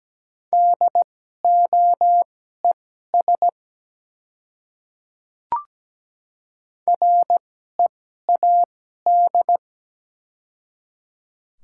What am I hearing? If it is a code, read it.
Morse code, heard as a keyed tone, about 17 wpm, spaced as for 12 wpm, 700 Hz. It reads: DOES READ
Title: 2x Character Spacing - Top 200 Words - 17wpm